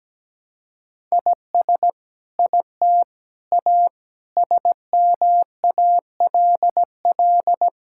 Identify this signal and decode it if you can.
Morse code, heard as a keyed tone, about 17 wpm, 700 Hz. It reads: IS IT A SMALL
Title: Sentences from Top 1000 Words - Review Part 5 - 17wpm